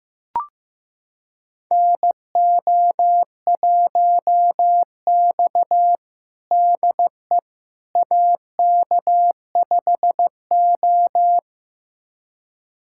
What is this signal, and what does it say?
Morse code, heard as a keyed tone, about 15 wpm, 700 Hz. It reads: NO1X DE AK5O
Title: USA Callsign DE Callsign - 4 Characters - 15wpm